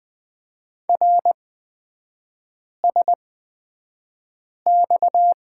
Morse code, heard as a keyed tone, about 20 wpm, spaced as for 5 wpm, 690 Hz. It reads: RSX
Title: 229 Lesson 37e - Letter x - Three Characters 20wpm